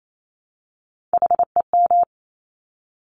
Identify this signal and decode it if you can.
Morse code, heard as a keyed tone, about 28 wpm, 700 Hz. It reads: HEM
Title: Top Three Letter Combinations in Words 28wpm